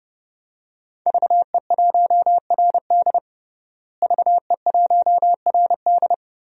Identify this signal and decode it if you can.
Morse code, heard as a keyed tone, about 30 wpm, 700 Hz. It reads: VE1RD VE1RD
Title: DXCC-100 Call Signs - 5 Characters - Repeated 3x - 30wpm